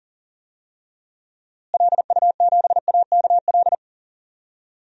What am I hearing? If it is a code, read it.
Morse code, heard as a keyed tone, about 40 wpm, 700 Hz. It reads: LU7AKL
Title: DXCC-100 Call Signs - 6 Characters - Repeated 3x - 40wpm